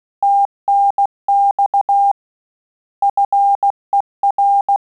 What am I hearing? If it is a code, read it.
Morse code, heard as a keyed tone, about 16 wpm, 800 Hz. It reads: TNX FER
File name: Telegraphy-group / QSO CW 15 PPM